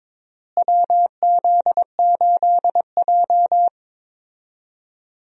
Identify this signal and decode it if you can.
Morse code, heard as a keyed tone, about 22 wpm, 700 Hz. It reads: WZ8J
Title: USA Call Signs - 4 Characters - Repeated 2x - 22wpm